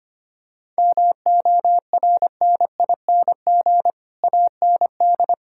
Morse code, heard as a keyed tone, about 25 wpm, 700 Hz. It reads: MORNING AND
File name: Sets of 9 Words - 25wpm